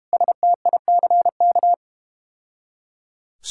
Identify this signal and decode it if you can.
Morse code, heard as a keyed tone, about 32 wpm, 700 Hz. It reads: STICK